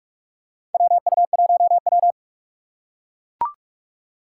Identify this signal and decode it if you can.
Morse code, heard as a keyed tone, about 44 wpm, 690 Hz. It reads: WU1W
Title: USA Call Signs - 4 Characters - 4x Character Spacing - 45wpm